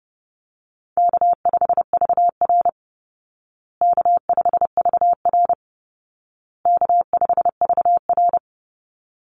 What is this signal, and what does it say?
Morse code, heard as a keyed tone, about 30 wpm, 700 Hz. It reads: K5VR K5VR K5VR
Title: Call Signs Repeated 2x - 30wpm